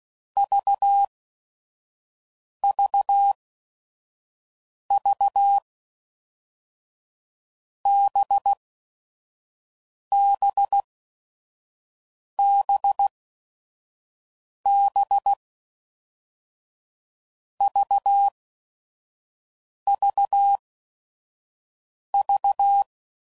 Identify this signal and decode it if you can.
Morse code, heard as a keyed tone, about 16 wpm, spaced as for 5 wpm, 800 Hz. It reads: VVV BBBB VVV
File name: Groups / Basic Letter Groups / VB46